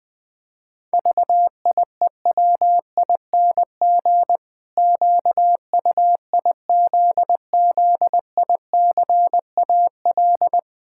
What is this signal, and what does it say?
Morse code, heard as a keyed tone, about 20 wpm, 700 Hz. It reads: VIEWING QUIZZICAL
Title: Pangram Challenge v2 - 20wpm